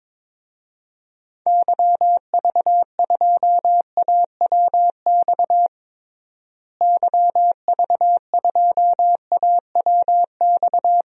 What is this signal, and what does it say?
Morse code, heard as a keyed tone, about 22 wpm, 700 Hz. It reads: YV2AWX YV2AWX
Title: DXCC-100 Call Signs - 6 Characters - Repeated 2x - 22wpm